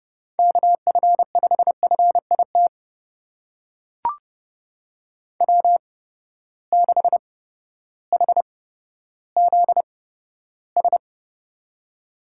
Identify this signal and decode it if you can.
Morse code, heard as a keyed tone, about 30 wpm, spaced as for 14 wpm, 700 Hz. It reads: KF5FIT W6HZS
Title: USA Call Signs - 8x Character Spacing - 30wpm